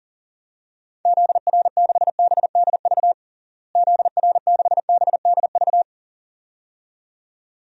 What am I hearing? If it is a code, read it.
Morse code, heard as a keyed tone, about 40 wpm, 700 Hz. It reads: ZR6BDV ZR6BDV
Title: DXCC-100 Call Signs - 6 Characters - Repeated 2x - 40wpm